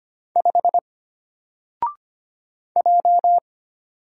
Morse code, heard as a keyed tone, about 25 wpm, 700 Hz. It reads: 5 J
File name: Single-Letter-Number-25wpm